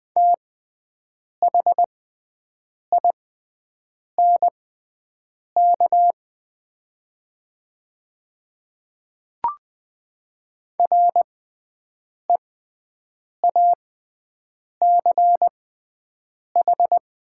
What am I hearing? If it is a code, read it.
Morse code, heard as a keyed tone, about 20 wpm, spaced as for 7 wpm, 700 Hz. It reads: THINK REACH
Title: 6x Character Spacing - Top 500 Words - 20wpm